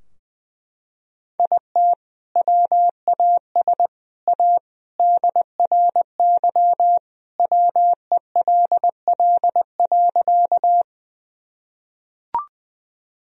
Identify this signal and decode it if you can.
Morse code, heard as a keyed tone, about 20 wpm, 700 Hz. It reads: IT WAS A DRY WELL.
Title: Sentences from Top 1000 Words - Review Part 3 - 20wpm